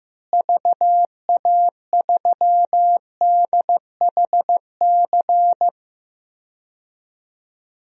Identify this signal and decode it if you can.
Morse code, heard as a keyed tone, about 16 wpm, 700 Hz. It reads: VA3DHC
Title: DXCC-100 Call Signs - 6 Characters - Repeated 2x - 15wpm